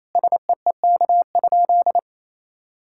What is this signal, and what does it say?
Morse code, heard as a keyed tone, about 28 wpm, 700 Hz. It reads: SEEK?
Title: Sentences from Top 1600 Words 28wpm